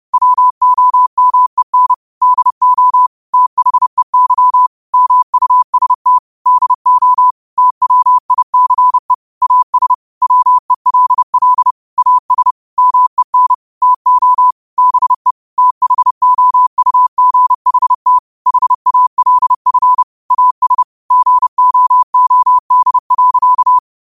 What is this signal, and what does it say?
Morse code, heard as a keyed tone, about 30 wpm, 1000 Hz. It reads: WOMEN DO THEY MUST DO TWICE AS WELL AS MEN TO BE THOUGHT HALF AS GOOD.
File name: qotdfiles / 30wpm